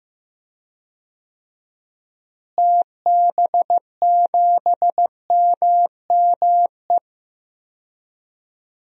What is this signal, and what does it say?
Morse code, heard as a keyed tone, about 16 wpm, 700 Hz. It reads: TB7MME
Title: DXCC-100 Call Signs - 6 Characters - Repeated 3x - 15wpm